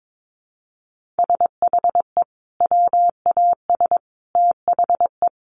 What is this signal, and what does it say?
Morse code, heard as a keyed tone, about 22 wpm, 700 Hz. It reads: SHE WAS THE
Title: Sentences from Top 1100 Words 22wpm